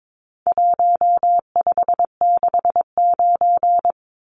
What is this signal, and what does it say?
Morse code, heard as a keyed tone, about 22 wpm, 700 Hz. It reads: 1569
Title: Numbers - Contest Style - 4-Digit 22wpm